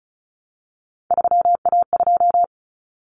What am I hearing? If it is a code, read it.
Morse code, heard as a keyed tone, about 34 wpm, 700 Hz. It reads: 3A2
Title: Common Cut Numbers - 3-Digit 35wpm